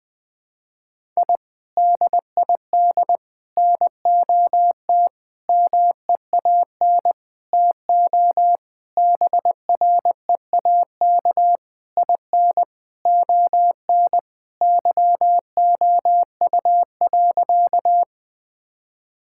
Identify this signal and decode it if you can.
Morse code, heard as a keyed tone, about 19 wpm, 700 Hz. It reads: I DID NOT MEAN TO BREAK IN ON YOU.
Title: Sentences from Top 1000 Words - Review Part 6 - 20wpm